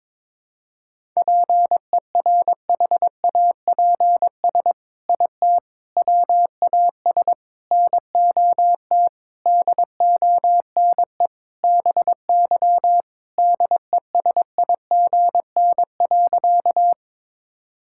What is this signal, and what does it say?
Morse code, heard as a keyed tone, about 22 wpm, 700 Hz. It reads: PERHAPS IT WAS NOT DONE BY DESIGN.
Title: Sentences from Top 1000 Words - Review Part 5 - 22wpm